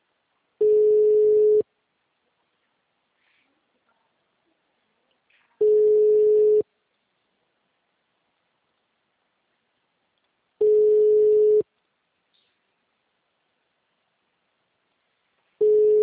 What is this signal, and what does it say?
Ringback tone (Europe, 425 Hz).
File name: out- -2015- - - .4580